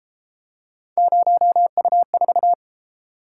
Morse code, heard as a keyed tone, about 29 wpm, 700 Hz. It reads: 0U4